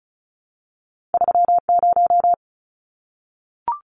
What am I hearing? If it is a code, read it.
Morse code, heard as a keyed tone, about 36 wpm, 700 Hz. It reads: 30
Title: Numbers - 2-Digit 35wpm